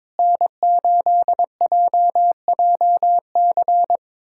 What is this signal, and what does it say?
Morse code, heard as a keyed tone, about 22 wpm, 700 Hz. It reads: N8JJC
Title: USA Callsign DE Callsign - 5 Characters - 22wpm